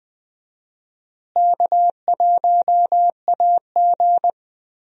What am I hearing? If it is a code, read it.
Morse code, heard as a keyed tone, about 20 wpm, 700 Hz. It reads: K1AG
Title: USA Call Signs - 4 Characters - Repeated 3x - 20wpm